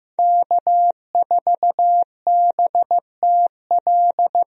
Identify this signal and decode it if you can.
Morse code, heard as a keyed tone, about 16 wpm, 700 Hz. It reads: K4BTL
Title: USA Call Signs - 5 Characters - Repeated 3x - 15wpm